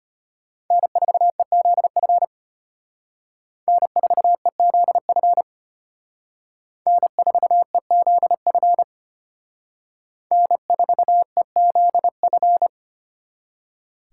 Morse code, heard as a keyed tone, about 32 wpm, 700 Hz. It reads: N4EZF N4EZF N4EZF N4EZF